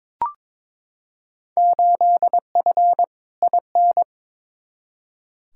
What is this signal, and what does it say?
Morse code, heard as a keyed tone, about 22 wpm, 700 Hz. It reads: 8F IN
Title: Field Day Exchanges Encore - 22wpm